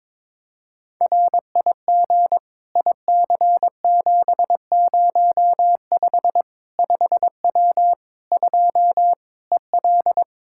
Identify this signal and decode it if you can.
Morse code, heard as a keyed tone, about 22 wpm, 700 Hz. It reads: RIG IC705 5W 2 EL
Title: K1USN SST QSO Practice 22wpm